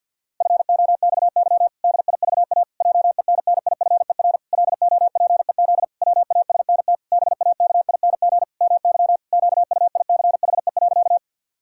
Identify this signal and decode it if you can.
Morse code, heard as a keyed tone, about 51 wpm, 700 Hz. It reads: FOXY DIVA JENNIFER LOPEZ WASNT BAKING MY QUICHE.
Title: Pangram Challenge v2 - 50wpm